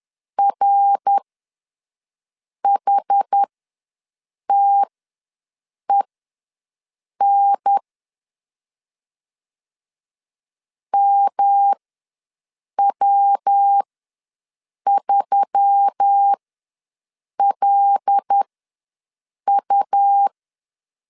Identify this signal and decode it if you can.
Morse code, heard as a keyed tone, about 10 wpm, spaced as for 6 wpm, 800 Hz. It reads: RHTEN MW3LU
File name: LEZIONI / 5 LBVC / 5 L ws 10wpm 3636